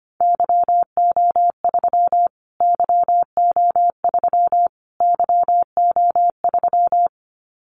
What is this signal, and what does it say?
Morse code, heard as a keyed tone, about 25 wpm, 700 Hz. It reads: YO3 YO3 YO3
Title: Callsign Prefixes for Top 25 DXCC - Repeated 3x - 25wpm